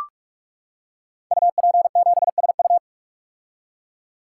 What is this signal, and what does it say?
Morse code, heard as a keyed tone, about 44 wpm, 700 Hz. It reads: UP7SU
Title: Code Groups - 5 Letter:Numbers - 45wpm